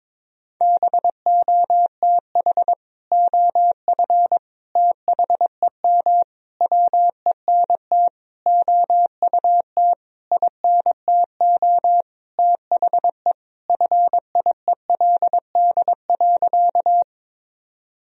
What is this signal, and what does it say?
Morse code, heard as a keyed tone, about 22 wpm, 700 Hz. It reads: BOTH OF THEM WENT OUT INTO THE FIELD.
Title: Sentences from Top 1000 Words - Review Part 2 - 22wpm